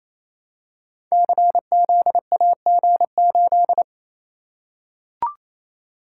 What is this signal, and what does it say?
Morse code, heard as a keyed tone, about 28 wpm, 700 Hz. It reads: CZAG8
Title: Code Groups - 5 Letter:Numbers - 28wpm